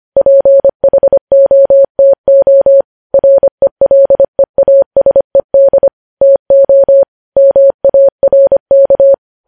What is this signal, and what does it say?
Morse code, heard as a keyed tone, about 25 wpm, 550 Hz. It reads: PHOTO RELEASED TO MARK
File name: podcast / News headlines-1x-25x25